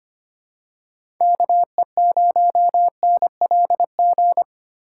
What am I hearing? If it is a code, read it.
Morse code, heard as a keyed tone, about 25 wpm, 700 Hz. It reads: KE0NLG
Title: USA Call Signs - 6 Characters - 8x Character Spacing - 25wpm